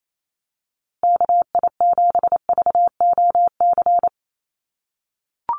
Morse code, heard as a keyed tone, about 28 wpm, 700 Hz. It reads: KI7VOC
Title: USA Call Signs Repeated 2x 28wpm